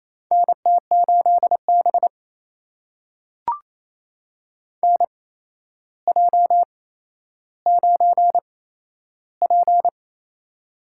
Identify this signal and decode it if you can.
Morse code, heard as a keyed tone, about 27 wpm, spaced as for 12 wpm, 700 Hz. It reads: NT8B NJ9P
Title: USA Call Signs - 4 Characters - 8x Character Spacing - 28wpm